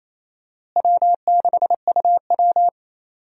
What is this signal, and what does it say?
Morse code, heard as a keyed tone, about 28 wpm, 700 Hz. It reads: W6UW
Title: USA Call Signs - 4 Characters - Repeated 3x - 28wpm